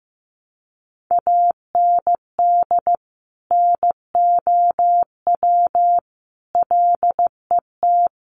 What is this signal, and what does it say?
Morse code, heard as a keyed tone, about 15 wpm, 700 Hz. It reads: AND NOW LET
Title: Sentences from Top 300 Words 15wpm